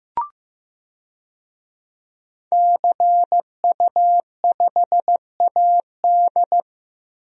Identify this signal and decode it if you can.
Morse code, heard as a keyed tone, about 15 wpm, 700 Hz. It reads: CU5AD
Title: DXCC-100 Call Signs - 5 Characters - Repeated 2x - 15wpm